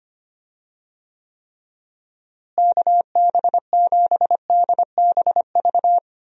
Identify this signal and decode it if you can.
Morse code, heard as a keyed tone, about 25 wpm, 700 Hz. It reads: KB7DBV
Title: USA Call Signs - 6 Characters - Repeated 2x - 25wpm